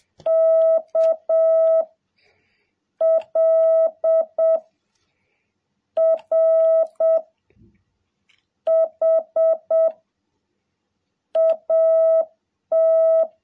Morse code, heard as a keyed tone, about 7 wpm, 660 Hz. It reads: K L R H AT
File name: jh8cbh / 2021cw